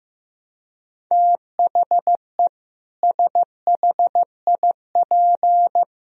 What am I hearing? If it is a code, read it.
Morse code, heard as a keyed tone, about 14 wpm, 700 Hz. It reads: THE SHIP
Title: Sentences from Top 1000 Words - Review Part 4 - 15wpm